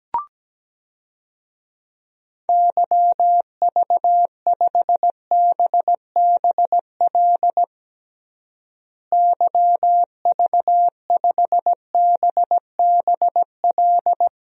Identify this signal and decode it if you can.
Morse code, heard as a keyed tone, about 17 wpm, 700 Hz. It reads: YV5BBL YV5BBL
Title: DXCC-100 Call Signs - 6 Characters - Repeated 3x - 17wpm